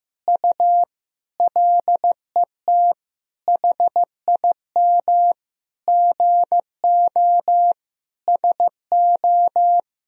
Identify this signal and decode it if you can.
Morse code, heard as a keyed tone, about 15 wpm, 700 Hz. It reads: U LET HIM GO SO
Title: Sentences from Top 300 Words and Abbreviations 15wpm